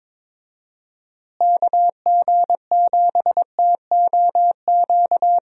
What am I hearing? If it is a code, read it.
Morse code, heard as a keyed tone, about 21 wpm, 700 Hz. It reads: KG7TOQ
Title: USA Call Signs - 6 Characters - Repeated 3x - 22wpm